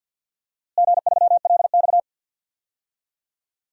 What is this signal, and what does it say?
Morse code, heard as a keyed tone, about 52 wpm, 700 Hz. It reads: G3LX
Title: DXCC-100 Call Signs - 4 Characters - Repeated 2x - 50wpm